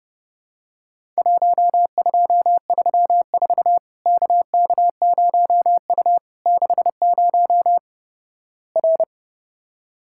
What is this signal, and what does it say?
Morse code, heard as a keyed tone, about 32 wpm, 700 Hz. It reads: 1234 KK0U 60 R
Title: Worked All Europe DX Contest - QTC Exchange - 30wpm